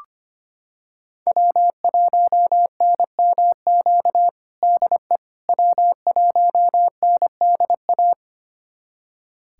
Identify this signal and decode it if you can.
Morse code, heard as a keyed tone, about 25 wpm, 700 Hz. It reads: W1NMQ DE W1NDA
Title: USA Callsign DE Callsign 25wpm